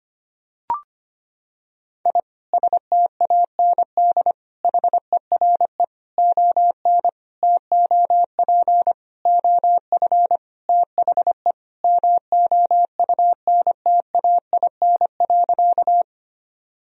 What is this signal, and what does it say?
Morse code, heard as a keyed tone, about 25 wpm, 700 Hz. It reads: I STAND HERE ON TOP OF THE MOUNTAIN.
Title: Sentences from Top 1000 Words - Review Part 1 - 25wpm